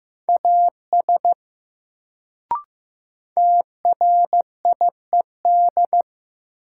Morse code, heard as a keyed tone, about 15 wpm, 700 Hz. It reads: AS TRIED
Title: Sets of 2 Words - New Word Drill - 15wpm